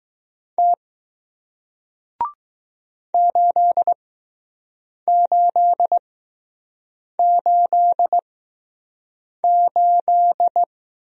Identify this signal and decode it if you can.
Morse code, heard as a keyed tone, about 20 wpm, spaced as for 6 wpm, 700 Hz. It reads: T 8888